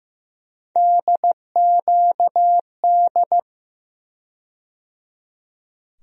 Morse code, heard as a keyed tone, about 15 wpm, 700 Hz. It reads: DQD
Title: USA Call Sign Postfixes - 3 Letters 15wpm